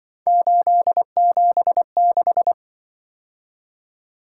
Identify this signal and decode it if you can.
Morse code, heard as a keyed tone, about 25 wpm, 700 Hz. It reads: 876
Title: Speed-Racing - Three Letter-Number Combinations - 20wpm